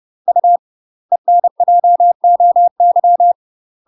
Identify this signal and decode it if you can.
Morse code, heard as a keyed tone, about 30 wpm, 700 Hz. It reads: U ENJOY
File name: UK MASTER QSOs TRAINING / 30WPM / UK MASTER QSOs 0013